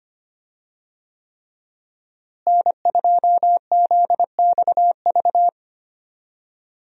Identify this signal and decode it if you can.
Morse code, heard as a keyed tone, about 25 wpm, 700 Hz. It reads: N2ZXV
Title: USA Call Signs - 5 Characters - Repeated 3x - 25wpm